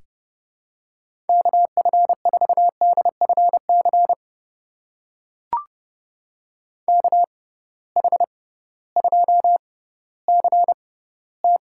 Morse code, heard as a keyed tone, about 30 wpm, spaced as for 16 wpm, 700 Hz. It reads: KF4DFC KH2CT
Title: USA Call Signs - 6x Character Spacing - 30wpm